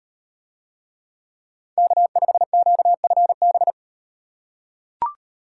Morse code, heard as a keyed tone, about 36 wpm, 700 Hz. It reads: K5QFB